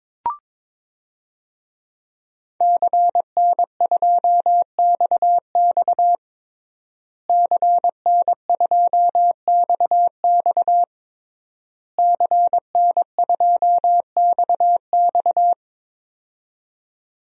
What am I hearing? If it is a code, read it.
Morse code, heard as a keyed tone, about 22 wpm, 700 Hz. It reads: CN2XX CN2XX CN2XX
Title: DXCC-100 Call Signs - 5 Characters - Repeated 3x - 22wpm